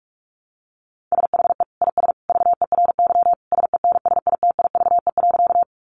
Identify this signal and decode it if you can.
Morse code, heard as a keyed tone, about 44 wpm, 700 Hz. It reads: SHE IS VERY SENSITIVE.
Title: Sentences from Top 1400 Words 45wpm